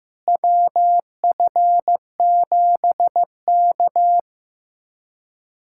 Morse code, heard as a keyed tone, about 15 wpm, 700 Hz. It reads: WF7K
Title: USA Callsign DE Callsign - 4 Characters - 15wpm